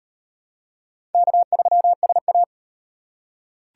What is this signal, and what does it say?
Morse code, heard as a keyed tone, about 34 wpm, 700 Hz. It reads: K3SA